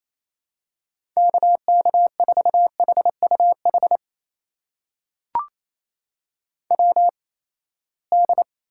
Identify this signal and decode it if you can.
Morse code, heard as a keyed tone, about 27 wpm, spaced as for 19 wpm, 700 Hz. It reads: KK4HUH WD
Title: USA Call Signs - 6 Characters - 8x Character Spacing - 28wpm